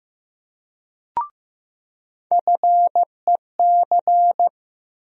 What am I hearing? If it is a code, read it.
Morse code, heard as a keyed tone, about 16 wpm, 700 Hz. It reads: FEC
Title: USA Call Sign Postfixes - 3 Letters 15wpm